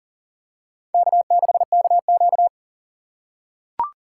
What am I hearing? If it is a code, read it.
Morse code, heard as a keyed tone, about 40 wpm, 700 Hz. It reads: K6KQ
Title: USA Call Signs - 4 Characters - 3x Character Spacing - 40wpm